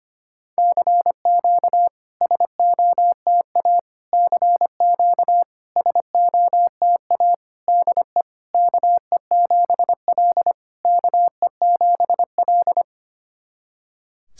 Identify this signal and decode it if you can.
Morse code, heard as a keyed tone, about 25 wpm, 700 Hz. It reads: CQ SOTA CQ SOTA DE KE7L KE7L
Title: SOTA QSO Practice 25wpm